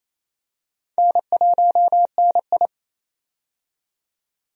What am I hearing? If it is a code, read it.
Morse code, heard as a keyed tone, about 28 wpm, 700 Hz. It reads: N1NI
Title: USA Call Signs - 4 Characters - 28wpm